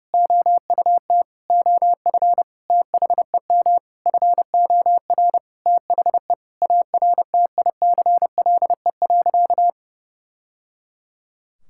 Morse code, heard as a keyed tone, about 32 wpm, 700 Hz. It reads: OUT OF THEM FOR THE ARTICLE.
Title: Sentences from Top 1000 Words - Review Part 5 - 30wpm